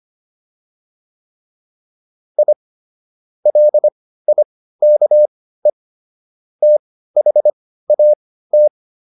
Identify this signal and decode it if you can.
Morse code, heard as a keyed tone, about 25 wpm, spaced as for 15 wpm, 600 Hz. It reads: I LIKE THAT